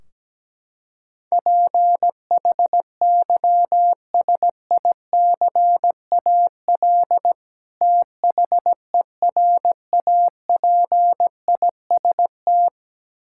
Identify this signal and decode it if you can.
Morse code, heard as a keyed tone, about 17 wpm, 700 Hz. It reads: PHYSICAL THERAPIST
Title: Practice QSO - Exchange 3 - More Info - 17wpm